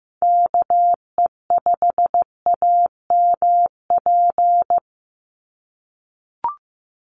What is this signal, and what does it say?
Morse code, heard as a keyed tone, about 15 wpm, 700 Hz. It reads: KE5AMP
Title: USA Call Signs 15wpm